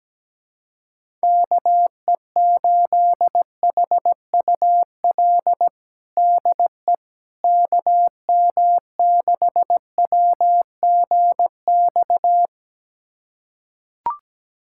Morse code, heard as a keyed tone, about 17 wpm, 700 Hz. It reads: KE8HUL DE KM6WGX
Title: USA Callsign DE Callsign 17wpm